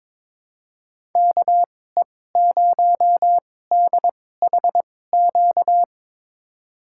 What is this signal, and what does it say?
Morse code, heard as a keyed tone, about 22 wpm, spaced as for 16 wpm, 700 Hz. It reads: KE0DHQ
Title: USA Call Signs - 2x Character Spacing - 22wpm